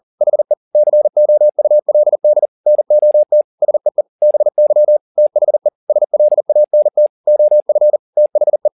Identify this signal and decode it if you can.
Morse code, heard as a keyed tone, about 40 wpm, 600 Hz. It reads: HE COULD NOT SEE BY THE SLANT OF THE
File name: the old man and the sea 2 40wpm